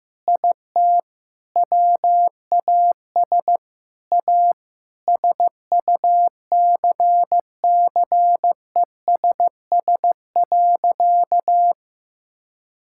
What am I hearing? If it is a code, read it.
Morse code, heard as a keyed tone, about 15 wpm, 700 Hz. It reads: IT WAS A SUCCESS.
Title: Sentences from Top 1000 Words - Review Part 6 - 15wpm